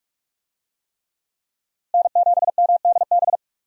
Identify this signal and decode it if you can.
Morse code, heard as a keyed tone, about 46 wpm, 700 Hz. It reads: N7MDB
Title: USA Call Signs - 5 Characters - Repeated 3x - 45wpm